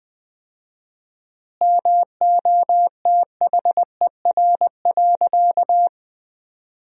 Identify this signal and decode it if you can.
Morse code, heard as a keyed tone, about 20 wpm, 700 Hz. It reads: MOTHER.
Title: 4x Word-Spacing - Sentences from Top 200 Words - 20wpm